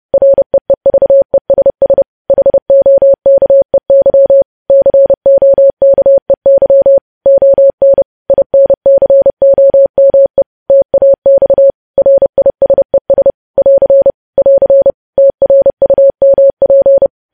Morse code, heard as a keyed tone, about 30 wpm, 550 Hz. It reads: REEVESS HOKEY COKEY ON INCOME TAX RISES <AR> <AR> TRUMP
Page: News Headlines in Morse Code at 30 WPM